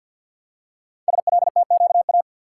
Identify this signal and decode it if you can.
Morse code, heard as a keyed tone, about 52 wpm, 700 Hz. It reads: SLTQA
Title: Code Groups - 5 Letters - 50wpm